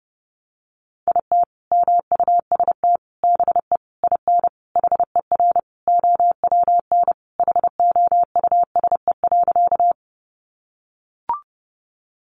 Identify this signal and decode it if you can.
Morse code, heard as a keyed tone, about 30 wpm, 700 Hz. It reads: IT MUST BE IN HER OWN HOUSE.
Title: Sentences from Top 300 Words 30wpm